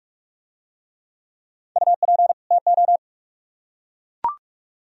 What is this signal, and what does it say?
Morse code, heard as a keyed tone, about 44 wpm, 700 Hz. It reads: UP TO